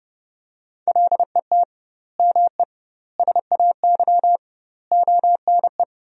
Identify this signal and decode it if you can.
Morse code, heard as a keyed tone, about 32 wpm, 700 Hz. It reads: LET ME SAY ONE
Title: 2x Word-Spacing - Sentences from Top 300 Words - 30wpm